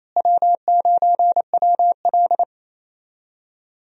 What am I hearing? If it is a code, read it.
Morse code, heard as a keyed tone, about 28 wpm, 700 Hz. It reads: W9WL
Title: USA Call Signs - 4 Characters - Repeated 3x - 28wpm